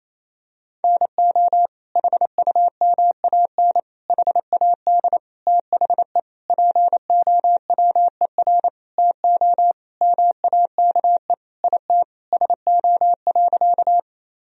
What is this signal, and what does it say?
Morse code, heard as a keyed tone, about 27 wpm, 700 Hz. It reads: NO HUMAN HAD THE POWER TO MAKE IT SO.
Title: Sentences from Top 1000 Words - Review Part 3 - 28wpm